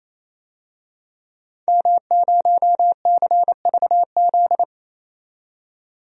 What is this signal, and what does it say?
Morse code, heard as a keyed tone, about 28 wpm, 700 Hz. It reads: M0CVZ
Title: DXCC-100 Call Signs - 5 Characters - 28wpm